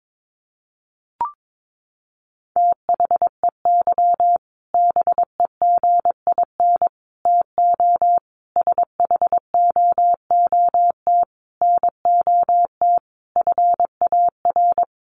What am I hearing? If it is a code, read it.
Morse code, heard as a keyed tone, about 22 wpm, 700 Hz. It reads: THEY BEGIN TO SHOOT NOT FAR
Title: Sentences from Top 1200 Words 22wpm